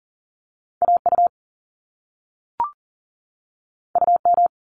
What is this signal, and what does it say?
Morse code, heard as a keyed tone, about 41 wpm, 700 Hz. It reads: AU UM
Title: Top Two Letter Combinations in Words 40wpm